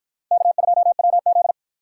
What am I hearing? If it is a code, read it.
Morse code, heard as a keyed tone, about 50 wpm, 700 Hz. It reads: K2WZ